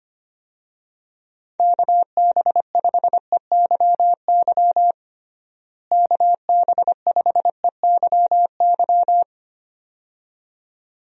Morse code, heard as a keyed tone, about 25 wpm, 700 Hz. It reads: KB5EYY KB5EYY
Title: USA Call Signs - 6 Characters - Repeated 2x - 25wpm